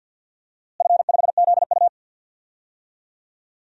Morse code, heard as a keyed tone, about 51 wpm, 700 Hz. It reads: F5ZU
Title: DXCC-100 Call Signs - 4 Characters - Repeated 3x - 50wpm